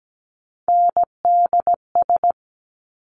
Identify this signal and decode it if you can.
Morse code, heard as a keyed tone, about 17 wpm, 700 Hz. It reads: NDS
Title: Top Three Letter Combinations in Words 17wpm